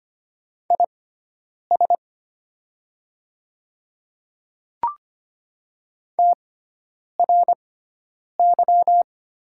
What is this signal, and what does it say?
Morse code, heard as a keyed tone, about 25 wpm, spaced as for 9 wpm, 700 Hz. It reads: IS TRY
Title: 6x Character Spacing - Top 200 Words - 25wpm